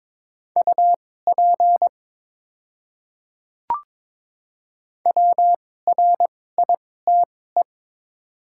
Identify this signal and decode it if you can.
Morse code, heard as a keyed tone, about 22 wpm, spaced as for 16 wpm, 700 Hz. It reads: UP WRITE
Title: 2x Character Spacing - Top 100 Words - 22wpm